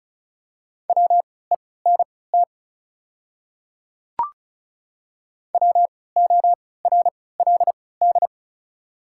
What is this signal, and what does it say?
Morse code, heard as a keyed tone, about 36 wpm, spaced as for 20 wpm, 700 Hz. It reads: WENT WORLD
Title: 3x Character Spacing - Top 200 Words - 35wpm